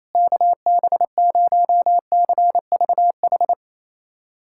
Morse code, heard as a keyed tone, about 28 wpm, 700 Hz. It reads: KB0CVH
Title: USA Callsign DE Callsign 28wpm